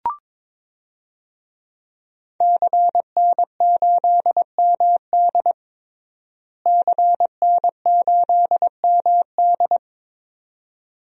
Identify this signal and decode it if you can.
Morse code, heard as a keyed tone, about 22 wpm, 700 Hz. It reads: CN8MD CN8MD
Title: DXCC-100 Call Signs - 5 Characters - Repeated 2x - 22wpm